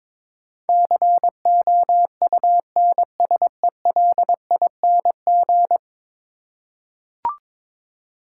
Morse code, heard as a keyed tone, about 21 wpm, 700 Hz. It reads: COUNSELING
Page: morse-code-ninja-practice-22wpm